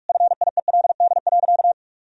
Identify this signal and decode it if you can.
Morse code, heard as a keyed tone, about 43 wpm, 700 Hz. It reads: FIELD.
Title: Sentences from Top 1000 Words - Review Part 2 - 45wpm